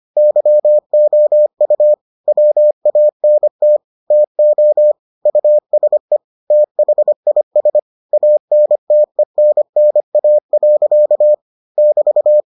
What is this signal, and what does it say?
Morse code, heard as a keyed tone, about 25 wpm, 600 Hz. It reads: YOU WANT TO USE THIS ANTENNA. <BT>